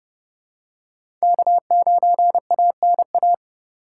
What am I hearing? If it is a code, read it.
Morse code, heard as a keyed tone, about 30 wpm, 700 Hz. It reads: K9ANA
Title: USA Call Signs - 5 Characters - 2x Character Spacing - 30wpm